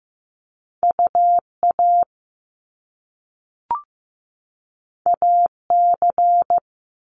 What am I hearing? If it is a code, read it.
Morse code, heard as a keyed tone, about 15 wpm, 700 Hz. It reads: UA AC
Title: Top Two Letter Combinations in Words 15wpm